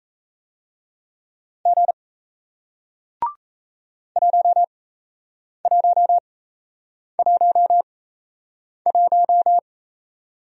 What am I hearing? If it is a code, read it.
Morse code, heard as a keyed tone, about 38 wpm, spaced as for 8 wpm, 700 Hz. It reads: G 1111